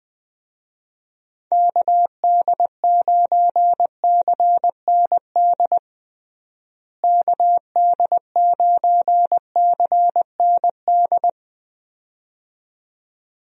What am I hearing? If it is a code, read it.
Morse code, heard as a keyed tone, about 20 wpm, 700 Hz. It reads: KD9CND KD9CND
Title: USA Call Signs - 6 Characters - Repeated 2x - 20wpm